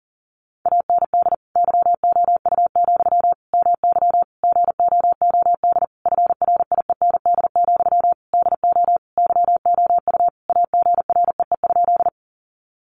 Morse code, heard as a keyed tone, about 40 wpm, 700 Hz. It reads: AND YOU, MY GOOD FRIEND, DO YOU AGREE?
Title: Sentences from Top 1100 Words 40wpm